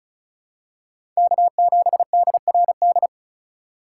Morse code, heard as a keyed tone, about 36 wpm, 700 Hz. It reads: K7DRD
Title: USA Call Signs - 5 Characters - 2x Character Spacing - 35wpm